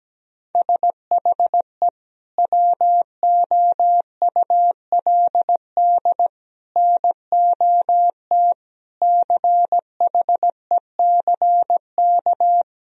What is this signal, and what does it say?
Morse code, heard as a keyed tone, about 17 wpm, 700 Hz. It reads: SHE WOULD NOT CHECK
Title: Sentences from Top 1000 Words - Review Part 3 - 17wpm